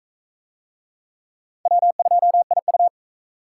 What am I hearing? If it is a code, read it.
Morse code, heard as a keyed tone, about 44 wpm, 700 Hz. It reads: W2IU